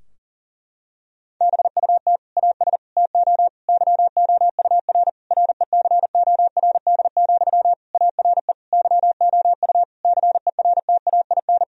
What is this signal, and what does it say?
Morse code, heard as a keyed tone, about 39 wpm, 700 Hz. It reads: BUT AS TO YOUR RECORD, ARE YOU CERTAIN
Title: Sentences from Top 1000 Words - Review Part 3 - 40wpm